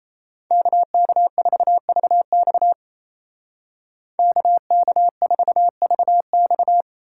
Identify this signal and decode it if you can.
Morse code, heard as a keyed tone, about 36 wpm, 700 Hz. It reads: KK4VX KK4VX